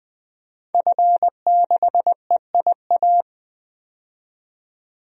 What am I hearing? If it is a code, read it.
Morse code, heard as a keyed tone, about 20 wpm, 700 Hz. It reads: F6EIA
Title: DXCC-100 Call Signs - 5 Characters - 20wpm